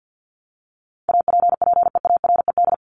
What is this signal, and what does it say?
Morse code, heard as a keyed tone, about 50 wpm, 700 Hz. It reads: APPEARED
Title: ED Suffix - Top 100 Words with Base Words 50wpm